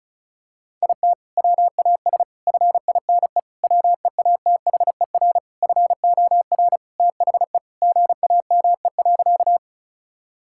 Morse code, heard as a keyed tone, about 36 wpm, 690 Hz. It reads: IT WAS FINE WEATHER FOR THE GAME.
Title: Sentences from Top 1000 Words - Review Part 5 - 35wpm